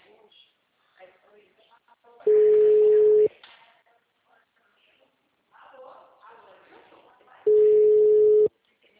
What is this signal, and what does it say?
Ringback tone (Europe, 425 Hz), with speech.
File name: out- -2011- - - .10457